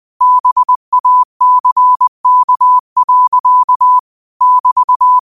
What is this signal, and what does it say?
Morse code, heard as a keyed tone, about 20 wpm, 1000 Hz. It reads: BACK. <BT>
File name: qotdfiles / 20wpm